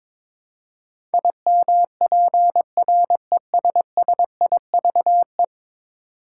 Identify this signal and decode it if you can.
Morse code, heard as a keyed tone, about 22 wpm, 700 Hz. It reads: IMPRESSIVE
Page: morse-code-ninja-practice-22wpm